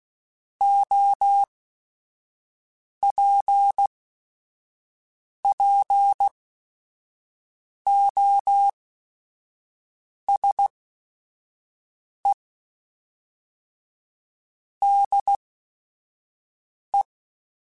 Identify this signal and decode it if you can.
Morse code, heard as a keyed tone, about 16 wpm, spaced as for 5 wpm, 800 Hz. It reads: OPPOSE DE
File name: Technical Subjects / Technical Subjects-mp3s / XC @5